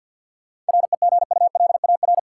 Morse code, heard as a keyed tone, about 47 wpm, 700 Hz. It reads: REGULAR
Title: Sentences from Top 900 Words - New Word Drill - 50wpm